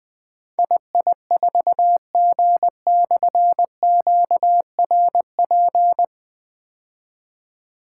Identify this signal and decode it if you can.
Morse code, heard as a keyed tone, about 21 wpm, 700 Hz. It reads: II4G/QRP
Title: Difficult Call Signs 20wpm